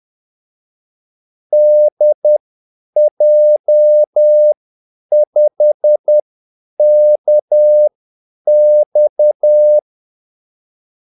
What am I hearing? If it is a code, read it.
Morse code, heard as a keyed tone, about 10 wpm, spaced as for 8 wpm, 600 Hz. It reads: DJ5KX
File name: Callsigns-10-8wpm